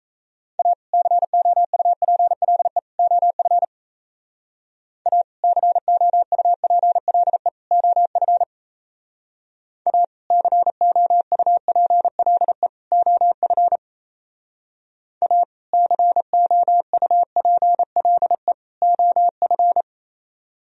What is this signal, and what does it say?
Morse code, heard as a keyed tone, about 33 wpm, 700 Hz. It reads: A COUPLE OF A COUPLE OF A COUPLE OF A COUPLE OF
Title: Speed-Racing - Sets of 3 Words 28wpm